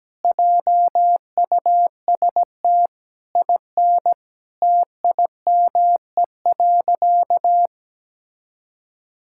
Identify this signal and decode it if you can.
Morse code, heard as a keyed tone, about 17 wpm, 700 Hz. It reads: JUST IN TIME.
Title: Sentences from Top 1600 Words 17wpm